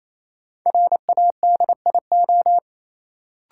Morse code, heard as a keyed tone, about 28 wpm, 700 Hz. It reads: RADIO
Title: Sentences from Top 1000 Words - Review Part 3 - New Word Drill - 28wpm